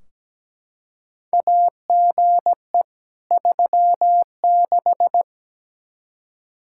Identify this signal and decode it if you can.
Morse code, heard as a keyed tone, about 17 wpm, 700 Hz. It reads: AGE 36
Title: Practice QSO - Exchange 3 - More Info - 17wpm